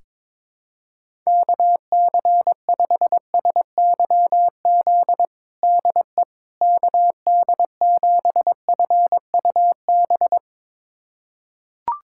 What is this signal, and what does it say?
Morse code, heard as a keyed tone, about 22 wpm, 700 Hz. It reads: KC5SYZ DE KD7FUB
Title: USA Callsign DE Callsign - 6 Characters - 22wpm